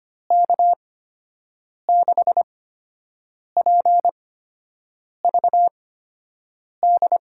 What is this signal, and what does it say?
Morse code, heard as a keyed tone, about 25 wpm, spaced as for 7 wpm, 700 Hz. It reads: K6PVD
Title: USA Call Signs - 6 Characters - 8x Character Spacing - 25wpm